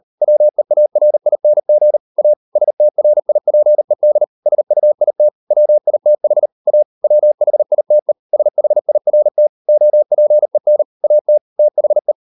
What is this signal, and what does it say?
Morse code, heard as a keyed tone, about 38 wpm, 600 Hz. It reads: WEARING A STRIPED SUIT WITH A WHITE SHIRT OPEN AT THE